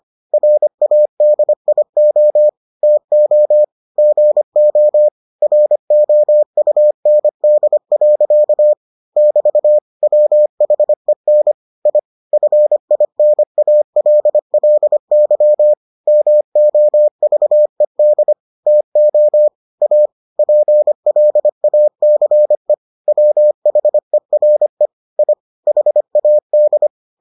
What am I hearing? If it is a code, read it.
Morse code, heard as a keyed tone, about 25 wpm, 600 Hz. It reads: RADIO TO GO ROUND. <BT> WHEN I FINALLY MOVED TO A PLACE WHERE I HAD